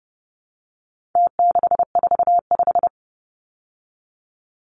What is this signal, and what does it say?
Morse code, heard as a keyed tone, about 30 wpm, 700 Hz. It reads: T645
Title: Numbers - Contest Style - 4-Digit 30wpm